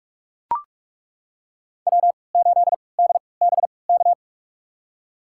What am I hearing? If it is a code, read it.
Morse code, heard as a keyed tone, about 42 wpm, spaced as for 25 wpm, 700 Hz. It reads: W8DBK
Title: USA Call Signs - 5 Characters - 3x Character Spacing - 45wpm